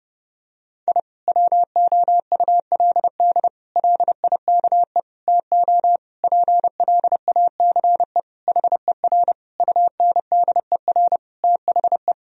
Morse code, heard as a keyed tone, about 32 wpm, 700 Hz. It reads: I WOULD LIKE TO PLACE HER UNDER THE
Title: Sentences from Top 1000 Words - Review Part 1 - 30wpm